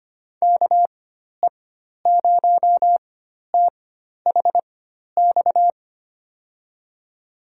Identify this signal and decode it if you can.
Morse code, heard as a keyed tone, about 25 wpm, spaced as for 12 wpm, 700 Hz. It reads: KE0THX
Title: USA Call Signs - 6 Characters - 4x Character Spacing - 25wpm